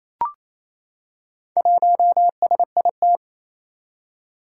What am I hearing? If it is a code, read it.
Morse code, heard as a keyed tone, about 29 wpm, 700 Hz. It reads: 1SIT
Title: Four Challenging Letter-Number Combinations 28wpm